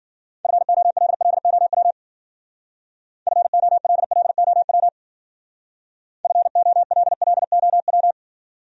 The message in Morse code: FOLLOW FOLLOW FOLLOW

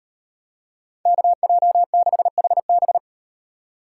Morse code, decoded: KJ6HB

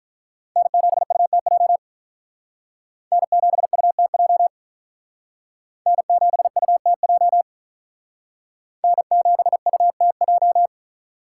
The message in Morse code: N7UTJ N7UTJ N7UTJ N7UTJ

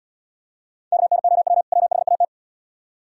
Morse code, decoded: KNOW WHAT

